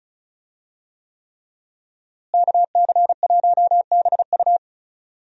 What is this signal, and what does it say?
Morse code, heard as a keyed tone, about 36 wpm, 700 Hz. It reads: KC1BU
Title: USA Call Signs - 5 Characters - Repeated 3x - 35wpm